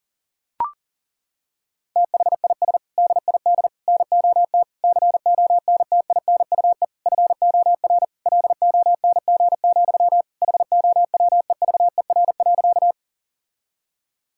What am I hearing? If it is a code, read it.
Morse code, heard as a keyed tone, about 37 wpm, 700 Hz. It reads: THIS DID NOT CONTINUE FOR LONG, HOWEVER.
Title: Sentences from Top 1000 Words - Review Part 6 - 40wpm